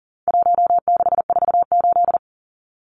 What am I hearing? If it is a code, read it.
Morse code, heard as a keyed tone, about 40 wpm, 700 Hz. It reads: 1648
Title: Numbers - 4-Digit 40wpm